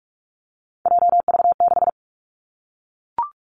Morse code, heard as a keyed tone, about 46 wpm, 700 Hz. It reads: JV6
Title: Three Challenging Letter-Number Combinations 45wpm